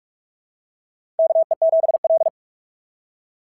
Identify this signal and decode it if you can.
Morse code, heard as a keyed tone, about 44 wpm, 690 Hz. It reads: KE7L
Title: SOTA QSO Practice 45wpm